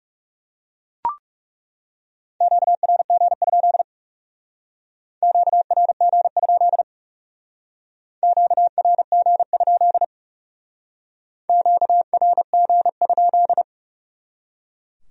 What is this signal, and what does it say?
Morse code, heard as a keyed tone, about 38 wpm, 700 Hz. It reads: QRG? QRG? QRG? QRG?